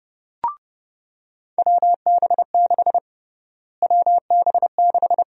Morse code, heard as a keyed tone, about 30 wpm, 700 Hz. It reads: WB6 WB6
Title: USA Call Sign Prefixes - Repeated 2x - 30wpm